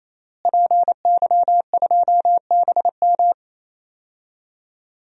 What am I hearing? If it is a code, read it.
Morse code, heard as a keyed tone, about 28 wpm, 700 Hz. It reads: PY2BM
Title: DXCC-100 Call Signs - 5 Characters - Repeated 3x - 28wpm